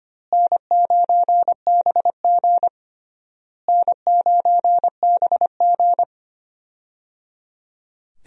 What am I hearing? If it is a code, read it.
Morse code, heard as a keyed tone, about 25 wpm, 700 Hz. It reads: N9BG N9BG
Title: USA Call Signs - 4 Characters - Repeated 3x - 25wpm